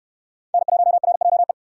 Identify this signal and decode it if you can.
Morse code, heard as a keyed tone, about 69 wpm, 700 Hz. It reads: N1RJE